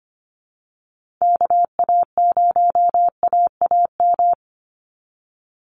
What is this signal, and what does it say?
Morse code, heard as a keyed tone, about 25 wpm, 700 Hz. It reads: KA0AAM
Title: Call Signs Repeated 3x - 25wpm